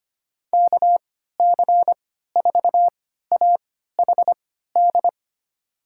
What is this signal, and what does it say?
Morse code, heard as a keyed tone, about 25 wpm, spaced as for 14 wpm, 700 Hz. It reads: KC4AHD
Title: USA Call Signs - 6 Characters - 3x Character Spacing - 25wpm